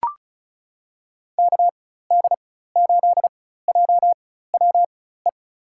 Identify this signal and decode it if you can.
Morse code, heard as a keyed tone, about 36 wpm, spaced as for 16 wpm, 700 Hz. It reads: KD8JWE
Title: USA Call Signs - 4x Character Spacing - 35wpm